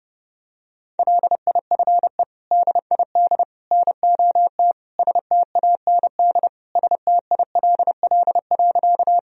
Morse code, heard as a keyed tone, about 32 wpm, 700 Hz. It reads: LIFE DID NOT STAND STILL.
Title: Sentences from Top 1000 Words - Review Part 1 - 30wpm